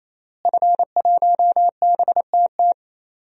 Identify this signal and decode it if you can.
Morse code, heard as a keyed tone, about 28 wpm, 700 Hz. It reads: F1BTT
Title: DXCC-100 Call Signs - 5 Characters - Repeated 3x - 28wpm